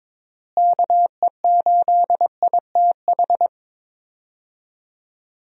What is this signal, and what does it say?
Morse code, heard as a keyed tone, about 22 wpm, 700 Hz. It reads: KE8ITH
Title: USA Call Signs - 6 Characters - Repeated 2x - 22wpm